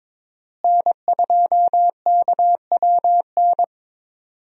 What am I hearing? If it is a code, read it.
Morse code, heard as a keyed tone, about 22 wpm, 700 Hz. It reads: N2KWN
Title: USA Call Signs - 5 Characters - Repeated 3x - 22wpm